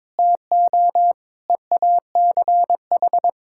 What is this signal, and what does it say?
Morse code, heard as a keyed tone, about 21 wpm, 700 Hz. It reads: TO EACH
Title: Sentences from Top 1000 Words - Review Part 5 - 22wpm